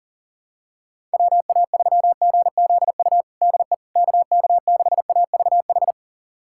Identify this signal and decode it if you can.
Morse code, heard as a keyed tone, about 40 wpm, 700 Hz. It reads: WA3GZU DE KK6AVH
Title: USA Callsign DE Callsign - 6 Characters - 40wpm